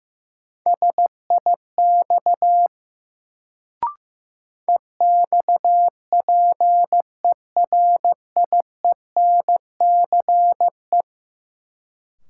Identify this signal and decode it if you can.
Morse code, heard as a keyed tone, about 15 wpm, 700 Hz. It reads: SIX EXPERIENCE
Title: Sentences from Top 600 Words - New Word Drill - 15wpm